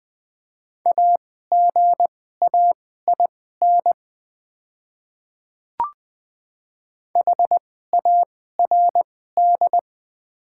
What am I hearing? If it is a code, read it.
Morse code, heard as a keyed tone, about 20 wpm, spaced as for 15 wpm, 700 Hz. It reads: AGAIN HARD
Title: 2x Character Spacing - Top 300 Words - 20wpm